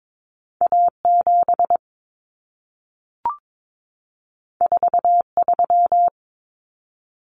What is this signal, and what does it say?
Morse code, heard as a keyed tone, about 22 wpm, 700 Hz. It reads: A7 43
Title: Common Cut Numbers - 2-Digit 22wpm